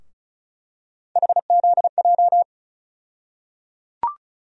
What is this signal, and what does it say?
Morse code, heard as a keyed tone, about 34 wpm, 700 Hz. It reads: HZJ
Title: USA Call Sign Postfixes - 3 Letters 35wpm